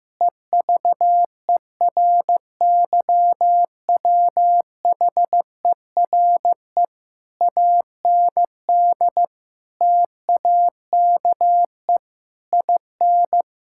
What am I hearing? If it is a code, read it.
Morse code, heard as a keyed tone, about 15 wpm, 700 Hz. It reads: EVERYWHERE AND TAKE IN
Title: Sentences from Top 1600 Words 15wpm